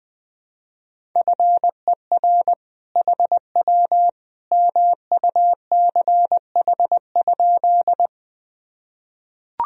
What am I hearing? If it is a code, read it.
Morse code, heard as a keyed tone, about 20 wpm, 700 Hz. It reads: FER HW MUCH?
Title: Sentences from Top 200 Words and Abbreviations 20wpm